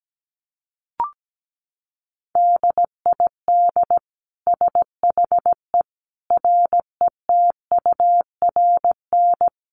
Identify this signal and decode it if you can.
Morse code, heard as a keyed tone, about 17 wpm, 700 Hz. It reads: DID SHE RETURN
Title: Sentences from Top 800 Words 17wpm